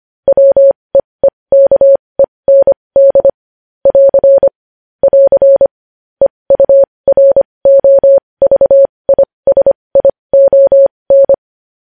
Morse code, heard as a keyed tone, about 25 wpm, spaced as for 20 wpm, 550 Hz. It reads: WEEKEND <AR> <AR> EUROVISION
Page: news-headlines-in-morse-code-at-20-wpm